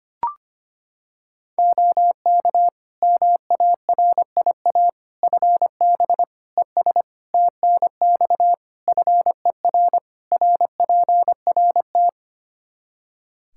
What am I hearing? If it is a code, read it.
Morse code, heard as a keyed tone, about 25 wpm, 700 Hz. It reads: OK MARIA FB ES TNX FER RPRT
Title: Practice QSO - Exchange 2 - Info - 25wpm